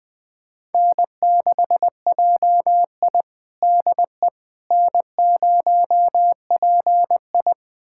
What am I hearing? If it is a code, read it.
Morse code, heard as a keyed tone, about 20 wpm, 700 Hz. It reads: N6JI DE N0PI
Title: USA Callsign DE Callsign - 4 Characters - 20wpm